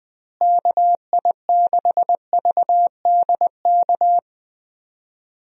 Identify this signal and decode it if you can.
Morse code, heard as a keyed tone, about 20 wpm, 700 Hz. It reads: KI6VDK
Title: USA Call Signs - 6 Characters - Repeated 2x - 20wpm